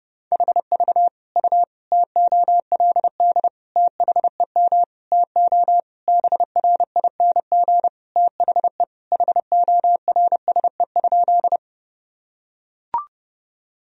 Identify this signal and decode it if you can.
Morse code, heard as a keyed tone, about 30 wpm, 700 Hz. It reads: HV U TOLD THEM TO BRING THE HORSE?
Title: Sentences from Top 500 Words and Abbreviations 30wpm